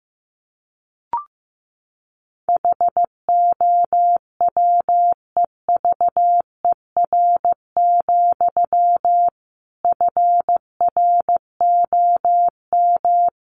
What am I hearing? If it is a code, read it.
Morse code, heard as a keyed tone, about 15 wpm, 700 Hz. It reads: HOWEVER, FROM
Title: Sentences from Top 1300 Words 15wpm